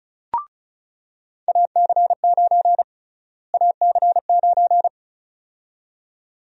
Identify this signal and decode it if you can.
Morse code, heard as a keyed tone, about 34 wpm, 700 Hz. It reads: AC9 AC9
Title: USA Call Sign Prefixes - Repeated 2x - 35wpm